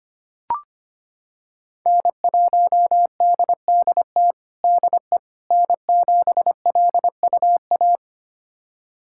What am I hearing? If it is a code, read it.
Morse code, heard as a keyed tone, about 25 wpm, 700 Hz. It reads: N1DDT DE N7LUA
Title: USA Callsign DE Callsign - 5 Characters - 25wpm